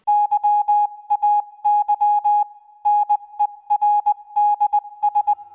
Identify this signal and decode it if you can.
Morse code, heard as a keyed tone, about 20 wpm, 830 Hz. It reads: YAY NERDS